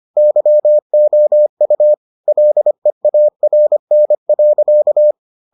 Morse code, heard as a keyed tone, about 25 wpm, 600 Hz. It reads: YOU LEARN.